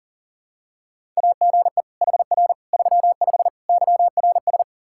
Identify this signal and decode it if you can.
Morse code, heard as a keyed tone, about 40 wpm, 700 Hz. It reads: AGE HR 35 YRS
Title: Practice QSO - Exchange 3 - More Info - 40wpm